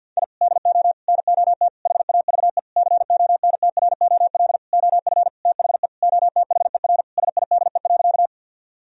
Morse code, heard as a keyed tone, about 50 wpm, 700 Hz. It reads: I DO NOT HAVE CONTROL OF THE OTHER SIDE.
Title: Sentences from Top 1000 Words - Review Part 3 - 50wpm